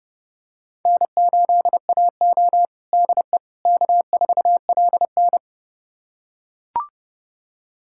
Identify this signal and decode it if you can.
Morse code, heard as a keyed tone, about 32 wpm, 700 Hz. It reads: N8AO DE K4LN
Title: USA Callsign DE Callsign - 4 Characters - 30wpm